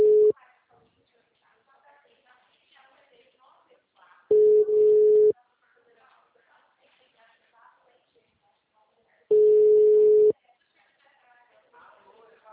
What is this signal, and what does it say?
Ringback tone (Europe, 425 Hz), with speech.